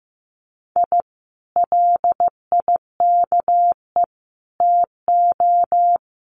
Morse code, heard as a keyed tone, about 15 wpm, 700 Hz. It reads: I LIKE TO
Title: Common QSO Phrases Encore 15wpm